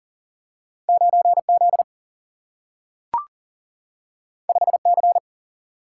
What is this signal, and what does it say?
Morse code, heard as a keyed tone, about 39 wpm, 700 Hz. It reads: 9Z 5C
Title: Single-Letter Single-Number 40wpm